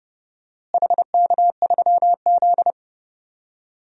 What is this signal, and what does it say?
Morse code, heard as a keyed tone, about 30 wpm, 700 Hz. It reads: HK3Z
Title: DXCC-100 Call Signs - 4 Characters - Repeated 2x - 30wpm